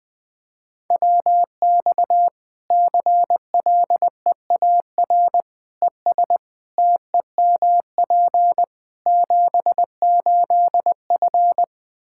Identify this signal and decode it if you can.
Morse code, heard as a keyed tone, about 20 wpm, 700 Hz. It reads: WX CLEAR ES TEMP 78F
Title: K1USN SST QSO Practice 20wpm